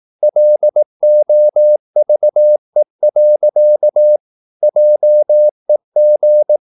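Morse code, heard as a keyed tone, about 18 wpm, 600 Hz. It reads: LOVE. JEG
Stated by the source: Slow (QRS) Morse practice at 18 wpm, CW med Gnister.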